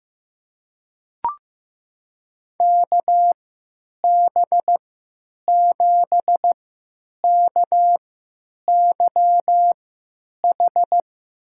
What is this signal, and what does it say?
Morse code, heard as a keyed tone, about 15 wpm, spaced as for 9 wpm, 700 Hz. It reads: KB7KYH
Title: USA Call Signs - 3x Character Spacing - 15wpm